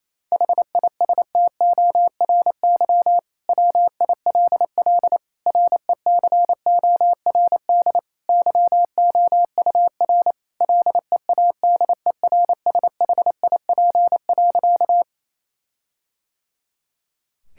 Morse code, heard as a keyed tone, about 28 wpm, 700 Hz. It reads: HISTORY WILL RECORD YOUR LEADERSHIP.
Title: Sentences from Top 1600 Words 28wpm